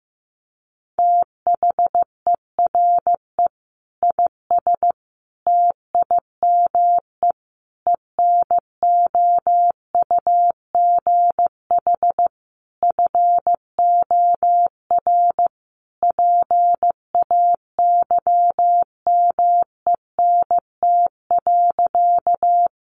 Morse code, heard as a keyed tone, about 15 wpm, 700 Hz. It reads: THERE IS TIME ENOUGH FOR PAYMENT.
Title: Sentences from Top 1200 Words 15wpm